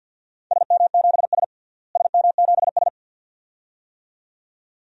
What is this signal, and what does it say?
Morse code, heard as a keyed tone, about 53 wpm, 700 Hz. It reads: SM7S SM7S
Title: DXCC-100 Call Signs - 4 Characters - Repeated 2x - 50wpm